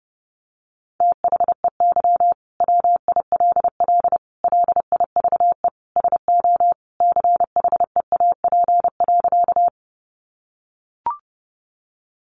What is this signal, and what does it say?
Morse code, heard as a keyed tone, about 30 wpm, 700 Hz. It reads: THEY WILL LIVE SO CHEAP.
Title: Sentences from Top 1200 Words 30wpm